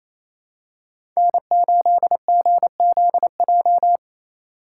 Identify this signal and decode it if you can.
Morse code, heard as a keyed tone, about 28 wpm, 700 Hz. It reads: N8GZJ
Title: USA Call Signs - 2x Character Spacing - 28wpm